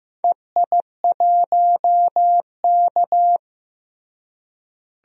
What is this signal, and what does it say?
Morse code, heard as a keyed tone, about 15 wpm, 700 Hz. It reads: EI1K
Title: DXCC-100 Call Signs - 4 Characters - Repeated 3x - 15wpm